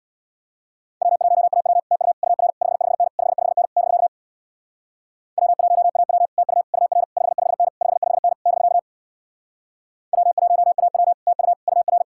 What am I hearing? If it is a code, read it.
Morse code, heard as a keyed tone, about 64 wpm, 700 Hz. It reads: W1AW TU UR 55N 55N <BK> W1AW TU UR 55N 55N <BK> W1AW TU UR